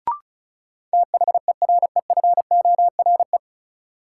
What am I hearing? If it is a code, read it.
Morse code, heard as a keyed tone, about 34 wpm, 700 Hz. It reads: THEREFORE
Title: Sentences from Top 800 Words - New Word Drill - 35wpm